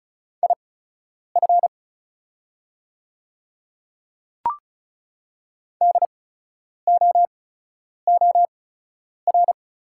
Morse code, heard as a keyed tone, about 36 wpm, spaced as for 10 wpm, 690 Hz. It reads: IF DOOR
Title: 8x Character Spacing - Top 500 Words - 35wpm